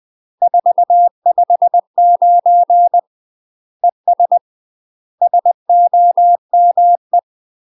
Morse code, heard as a keyed tone, about 20 wpm, 700 Hz. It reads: 459 ES SOME
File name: UK MASTER QSOs TRAINING / 20WPM / UK MASTER QSOs 0068